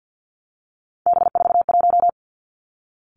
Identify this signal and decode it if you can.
Morse code, heard as a keyed tone, about 50 wpm, 700 Hz. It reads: B41
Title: Three Challenging Letter-Number Combinations 50wpm